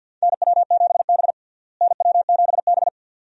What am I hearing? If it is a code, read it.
Morse code, heard as a keyed tone, about 46 wpm, 700 Hz. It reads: NW7B NW7B
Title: USA Call Signs - 4 Characters - Repeated 2x - 50wpm